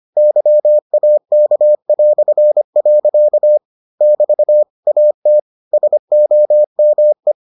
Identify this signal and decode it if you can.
Morse code, heard as a keyed tone, about 25 wpm, 600 Hz. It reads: YAK". <BT> AT SOME